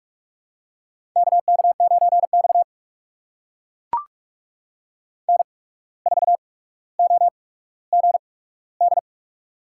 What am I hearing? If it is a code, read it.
Morse code, heard as a keyed tone, about 44 wpm, spaced as for 18 wpm, 700 Hz. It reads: KK9X N4OGD
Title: USA Call Signs - 8x Character Spacing - 45wpm